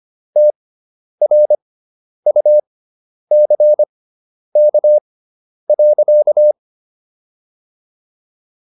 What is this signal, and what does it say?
Morse code, heard as a keyed tone, about 25 wpm, spaced as for 10 wpm, 600 Hz. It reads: TRUCK.